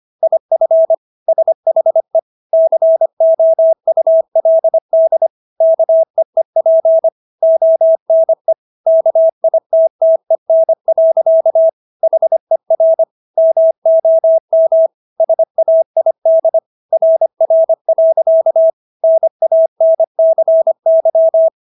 Morse code, heard as a keyed tone, about 25 wpm, 650 Hz. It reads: IF SHE COULD KEEP ONE KITTEN. HER MOM SAID RR. NANCY